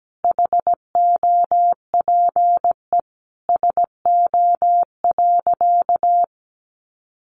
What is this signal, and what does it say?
Morse code, heard as a keyed tone, about 17 wpm, 700 Hz. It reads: HOPE SO.
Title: Sentences from Top 1000 Words 17wpm